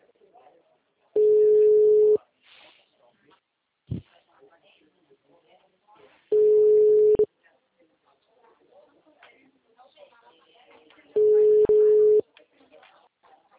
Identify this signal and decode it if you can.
Ringback tone (Europe, 425 Hz), with speech.